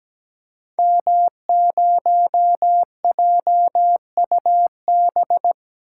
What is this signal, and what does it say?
Morse code, heard as a keyed tone, about 17 wpm, 700 Hz. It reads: M0JUB
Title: Code Groups - 5 Letter:Numbers - 17wpm